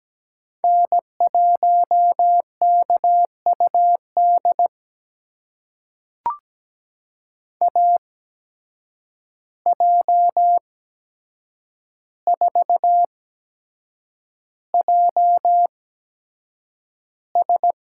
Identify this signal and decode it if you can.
Morse code, heard as a keyed tone, about 17 wpm, spaced as for 7 wpm, 700 Hz. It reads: N1KUD AJ4JS
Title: USA Call Signs - 5 Characters - 8x Character Spacing - 17wpm